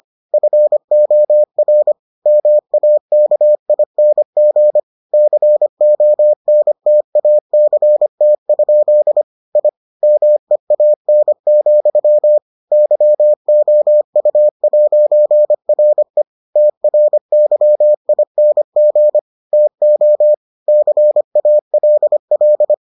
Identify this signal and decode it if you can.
Morse code, heard as a keyed tone, about 25 wpm, 600 Hz. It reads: FOR MAKING CONTACT? I MEAN, YOU'RE TRYING TO CALL